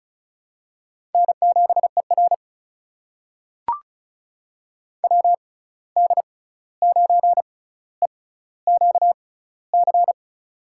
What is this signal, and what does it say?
Morse code, heard as a keyed tone, about 36 wpm, spaced as for 16 wpm, 700 Hz. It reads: N7ER WD9EQC
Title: USA Call Signs - 6x Character Spacing - 35wpm